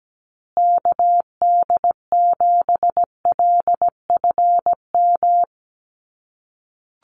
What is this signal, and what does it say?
Morse code, heard as a keyed tone, about 17 wpm, 700 Hz. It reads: KD7LFM
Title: USA Call Signs Repeated 2x 17wpm